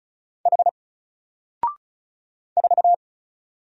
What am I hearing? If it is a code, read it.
Morse code, heard as a keyed tone, about 33 wpm, 700 Hz. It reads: H 4
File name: Single-Letter-Number-35wpm